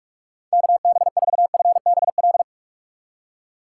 Morse code, heard as a keyed tone, about 46 wpm, 700 Hz. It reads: KB4FBL
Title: USA Call Signs - 3x Character Spacing - 45wpm